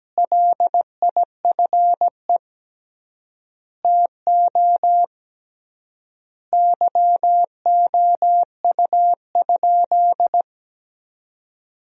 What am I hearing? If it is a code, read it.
Morse code, heard as a keyed tone, about 17 wpm, 700 Hz. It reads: LIFE TO YOU?
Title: 3x Word-Spacing - Sentences from Top 300 Words - 17wpm